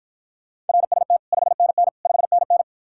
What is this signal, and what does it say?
Morse code, heard as a keyed tone, about 54 wpm, 700 Hz. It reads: RST 5NN 5NN